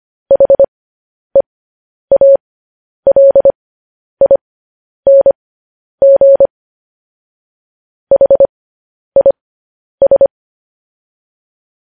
Morse code, heard as a keyed tone, about 25 wpm, spaced as for 10 wpm, 550 Hz. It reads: HEALING HIS